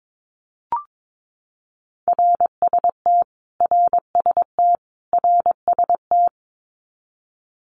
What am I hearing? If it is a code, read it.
Morse code, heard as a keyed tone, about 22 wpm, 700 Hz. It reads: RST RST RST
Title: Abbreviations Repeated 3x - 22wpm